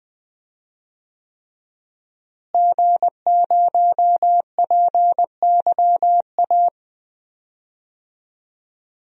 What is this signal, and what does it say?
Morse code, heard as a keyed tone, about 20 wpm, 700 Hz. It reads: G0PYA
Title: DXCC-100 Call Signs - 5 Characters - Repeated 3x - 20wpm